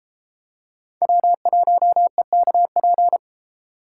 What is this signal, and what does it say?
Morse code, heard as a keyed tone, about 31 wpm, 700 Hz. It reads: W1EKP